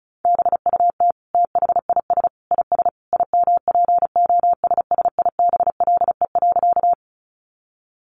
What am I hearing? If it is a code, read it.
Morse code, heard as a keyed tone, about 34 wpm, 700 Hz. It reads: BUT THIS IS IMPOSSIBLE.
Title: Sentences from Top 1100 Words 35wpm